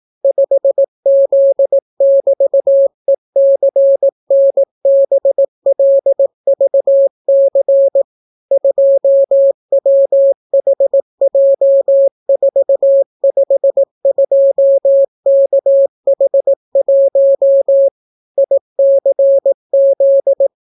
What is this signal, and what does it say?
Morse code, heard as a keyed tone, about 18 wpm, 550 Hz. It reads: 5Z<BT>ECNBLVC 2WHJ452KH1 ICZ